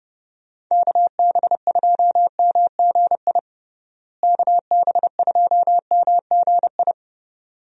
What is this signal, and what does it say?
Morse code, heard as a keyed tone, about 30 wpm, 700 Hz. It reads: KB2MGI KB2MGI
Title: USA Call Signs - 6 Characters - Repeated 2x - 30wpm